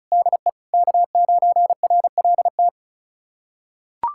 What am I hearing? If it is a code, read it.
Morse code, heard as a keyed tone, about 34 wpm, 700 Hz. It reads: DE K9RLT
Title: USA Callsign DE Callsign - 5 Characters - 35wpm